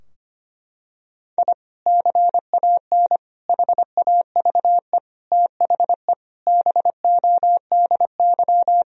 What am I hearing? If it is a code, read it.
Morse code, heard as a keyed tone, about 25 wpm, 700 Hz. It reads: I CAN HAVE THE BODY
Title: Sentences from Top 1000 Words - Review Part 2 - 25wpm